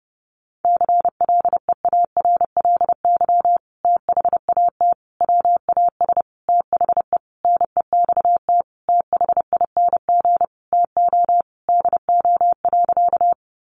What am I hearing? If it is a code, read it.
Morse code, heard as a keyed tone, about 30 wpm, 700 Hz. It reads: CLEARLY THAT WAS THE NEXT THING TO DO.
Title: Sentences from Top 1300 Words 30wpm